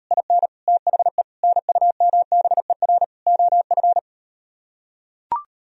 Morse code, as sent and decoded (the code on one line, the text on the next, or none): .. -. / - .... / -. ..- -- -... .-. / --- ..-.
IN THE NUMBER OF